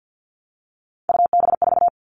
Morse code, heard as a keyed tone, about 50 wpm, 700 Hz. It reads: UB4
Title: All Cut Numbers - 3-Digit 50wpm